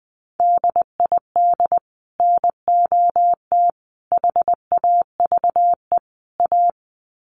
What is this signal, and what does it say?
Morse code, heard as a keyed tone, about 20 wpm, 700 Hz. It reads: DID NOT HAVE A
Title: Sets of 4 Words 20wpm